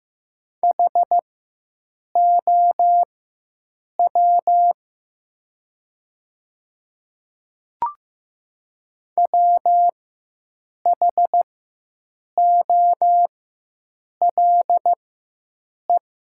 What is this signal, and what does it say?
Morse code, heard as a keyed tone, about 15 wpm, spaced as for 7 wpm, 690 Hz. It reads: HOW WHOLE
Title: 4x Character Spacing - Top 500 Words - 15wpm